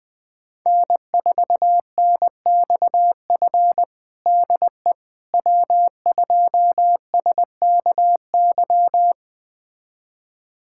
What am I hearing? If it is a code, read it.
Morse code, heard as a keyed tone, about 21 wpm, 700 Hz. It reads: N4NXF DE W2SKY
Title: USA Callsign DE Callsign - 5 Characters - 20wpm